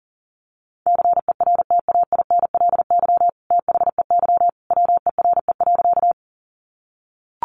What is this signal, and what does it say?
Morse code, heard as a keyed tone, about 40 wpm, 700 Hz. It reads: CERTAINLY THEY WERE.
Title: Sentences from Top 900 Words 40wpm